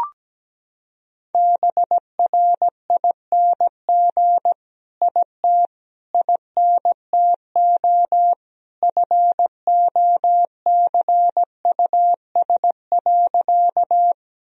Morse code, heard as a keyed tone, about 17 wpm, 700 Hz. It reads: BRING IT INTO FOCUS.
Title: Sentences from Top 1000 Words - Review Part 4 - 17wpm